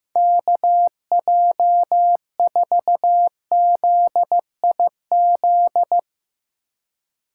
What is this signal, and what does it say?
Morse code, heard as a keyed tone, about 16 wpm, 700 Hz. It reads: KJ4ZIZ
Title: Speed Racing - USA 2x3 Call Signs 15wpm